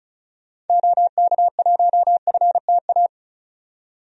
Morse code, heard as a keyed tone, about 34 wpm, 700 Hz. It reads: OK1FTA
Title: DXCC-100 Call Signs - 6 Characters - 35wpm